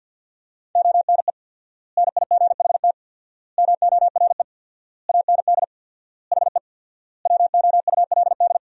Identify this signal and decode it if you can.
Morse code, heard as a keyed tone, about 52 wpm, 700 Hz. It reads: ONE NIGHT MORE AND HE WOULD
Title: 4x Word-Spacing - Sentences from Top 300 Words - 50wpm